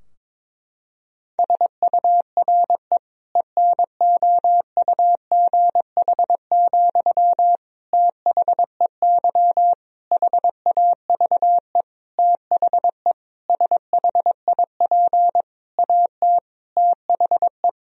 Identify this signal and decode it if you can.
Morse code, heard as a keyed tone, about 22 wpm, 700 Hz. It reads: SURE ENOUGH, THEY HAVE THE SHIP AT THE
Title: Sentences from Top 1000 Words - Review Part 2 - 22wpm